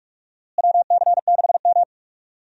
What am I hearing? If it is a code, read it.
Morse code, heard as a keyed tone, about 44 wpm, 700 Hz. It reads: WC6M
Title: USA Call Signs - 4 Characters - Repeated 2x - 45wpm